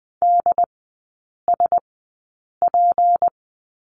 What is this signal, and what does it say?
Morse code, heard as a keyed tone, about 19 wpm, spaced as for 8 wpm, 700 Hz. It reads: DSP
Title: 205 Lesson 33e - Letter j - Three Characters 20wpm